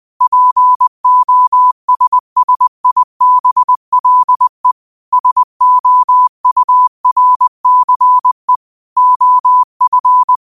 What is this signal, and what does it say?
Morse code, heard as a keyed tone, about 20 wpm, 1000 Hz. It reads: POSSIBLE SOURCE OF